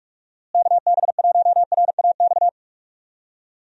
Morse code, heard as a keyed tone, about 46 wpm, 700 Hz. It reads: KB1RAX
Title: USA Call Signs - 6 Characters - Repeated 2x - 45wpm